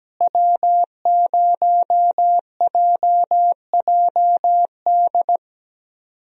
Morse code, heard as a keyed tone, about 17 wpm, 700 Hz. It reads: W0JJD
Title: USA Call Signs - 5 Characters - Repeated 3x - 17wpm